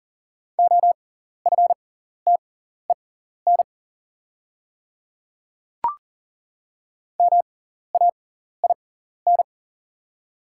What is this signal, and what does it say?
Morse code, heard as a keyed tone, about 40 wpm, spaced as for 14 wpm, 690 Hz. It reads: OFTEN MAIN
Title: 6x Character Spacing - Top 300 Words - 40wpm